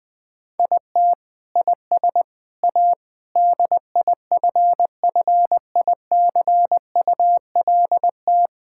Morse code, heard as a keyed tone, about 20 wpm, 700 Hz. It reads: IT IS A DIFFICULT
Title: Sentences from Top 1000 Words - Review Part 5 - 20wpm